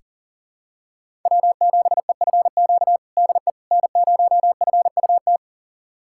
Morse code, heard as a keyed tone, about 40 wpm, 700 Hz. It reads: W7EFQ DE N0FUT
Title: USA Callsign DE Callsign - 5 Characters - 40wpm